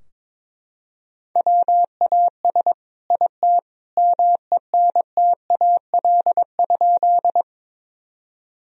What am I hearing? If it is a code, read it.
Morse code, heard as a keyed tone, about 22 wpm, 700 Hz. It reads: WAS IT MENTAL?
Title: Sentences from Top 1000 Words - Review Part 4 - 22wpm